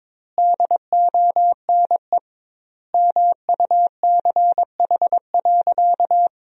Morse code, heard as a keyed tone, about 22 wpm, 700 Hz. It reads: DONE MUCH.
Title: 2x Word-Spacing - Sentences from Top 500 Words - 22wpm